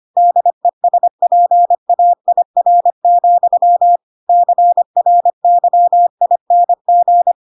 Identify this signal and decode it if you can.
Morse code, heard as a keyed tone, about 25 wpm, 700 Hz. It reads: DESPAIR, CRYING
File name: Pinocchio / Pinocchio ch7 25 25 ik1hge